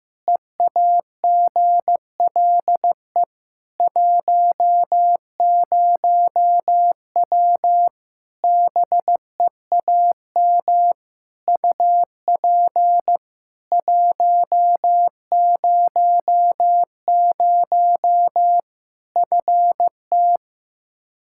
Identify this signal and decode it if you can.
Morse code, heard as a keyed tone, about 15 wpm, 700 Hz. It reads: EAGLE 10W BEAM UP 100 FT
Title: Practice QSO - Exchange 2 - Info - 15wpm